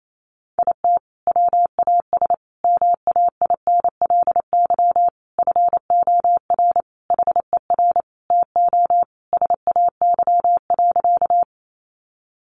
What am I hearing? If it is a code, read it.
Morse code, heard as a keyed tone, about 28 wpm, 700 Hz. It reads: IT WAS MAINLY FOR HER TO SAY.
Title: Sentences from Top 1400 Words 28wpm